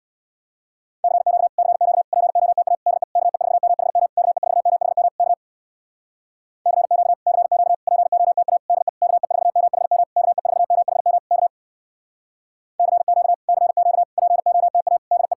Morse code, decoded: CQ CQ POTA DE KE4MHM KE4MHM K CQ CQ POTA DE KE4MHM KE4MHM K CQ CQ POTA DE

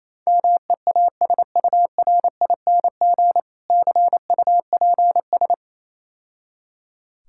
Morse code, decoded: MEASURING CUPS

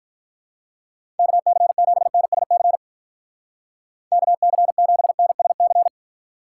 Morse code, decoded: KC7NSC KC7NSC

